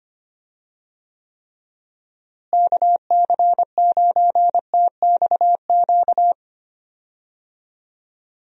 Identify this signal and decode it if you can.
Morse code, heard as a keyed tone, about 25 wpm, 700 Hz. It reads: KC9TXQ
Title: USA Call Signs - 6 Characters - Repeated 2x - 25wpm